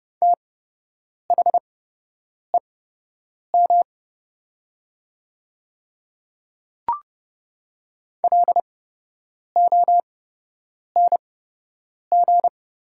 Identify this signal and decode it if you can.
Morse code, heard as a keyed tone, about 31 wpm, spaced as for 8 wpm, 700 Hz. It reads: THEM LONG
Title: 8x Character Spacing - Top 100 Words - 30wpm